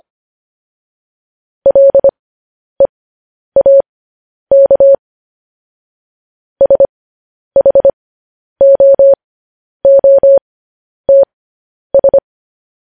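Morse code, heard as a keyed tone, about 25 wpm, spaced as for 10 wpm, 550 Hz. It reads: LEAK SHOOTS
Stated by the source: Morse code practice of news headlines at 10 wpm.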